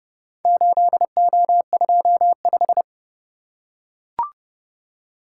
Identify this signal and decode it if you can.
Morse code, heard as a keyed tone, about 29 wpm, 700 Hz. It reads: 8O25